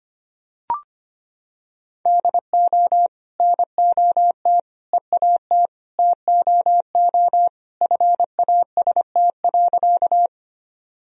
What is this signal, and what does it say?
Morse code, heard as a keyed tone, about 25 wpm, 700 Hz. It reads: DO NOT EAT TOO FAST.
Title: Sentences from Top 1000 Words - Review Part 2 - 25wpm